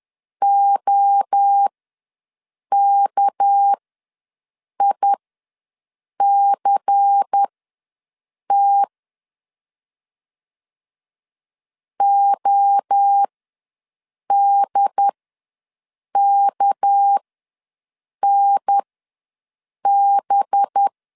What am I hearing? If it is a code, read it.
Morse code, heard as a keyed tone, about 10 wpm, spaced as for 6 wpm, 800 Hz. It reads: OKICT ODKNB
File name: LEZIONI / 6 PKX0 / 6 PK ws 10wpm 3745